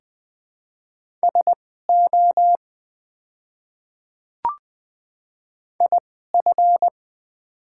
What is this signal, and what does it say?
Morse code, heard as a keyed tone, about 20 wpm, spaced as for 14 wpm, 700 Hz. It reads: SO IF
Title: 2x Character Spacing - Top 100 Words - 20wpm